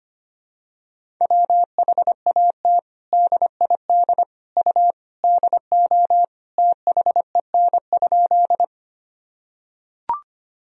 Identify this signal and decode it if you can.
Morse code, heard as a keyed tone, about 25 wpm, 700 Hz. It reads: WHAT DID U DO THEN?
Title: Sentences from Top 100 Words and Abbreviations 25wpm